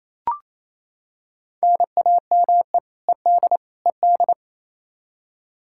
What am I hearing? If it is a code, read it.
Morse code, heard as a keyed tone, about 28 wpm, 700 Hz. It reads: NAME ED ED
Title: Practice QSO - Exchange 1 - Beginning - 28wpm